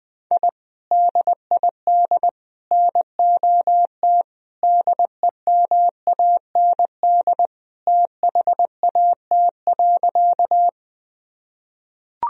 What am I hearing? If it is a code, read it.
Morse code, heard as a keyed tone, about 21 wpm, 700 Hz. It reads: I DID NOT DEMAND THAT.
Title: Sentences from Top 1000 Words - Review Part 4 - 20wpm